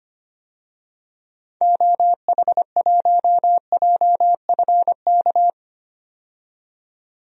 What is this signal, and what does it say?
Morse code, heard as a keyed tone, about 25 wpm, 700 Hz. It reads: OH1JFK
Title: DXCC-100 Call Signs - 6 Characters - 25wpm